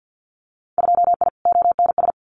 Morse code, heard as a keyed tone, about 50 wpm, 700 Hz. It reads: 2I ONS
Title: Winter Field Day Exchanges Encore 50wpm